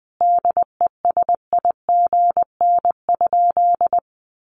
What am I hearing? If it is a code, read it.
Morse code, heard as a keyed tone, about 20 wpm, 700 Hz. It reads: DESIGN?
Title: Sentences from Top 800 Words 20wpm